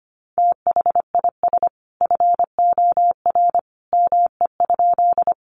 Morse code, heard as a keyed tone, about 25 wpm, 700 Hz. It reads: THIS FOR ME?
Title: Sentences from Top 200 Words 25wpm